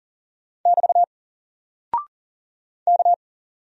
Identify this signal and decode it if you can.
Morse code, heard as a keyed tone, about 40 wpm, 700 Hz. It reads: <BT> K
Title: Single Character (Letter:Number:Punctuation:Prosign) 40wpm